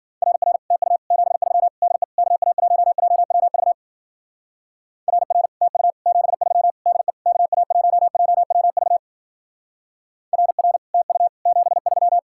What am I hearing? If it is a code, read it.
Morse code, heard as a keyed tone, about 49 wpm, 700 Hz. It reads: RR TU 73 DE KA1JWV RR TU 73 DE KA1JWV RR TU 73
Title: Speed-Racing Common QSO Phrases - 40wpm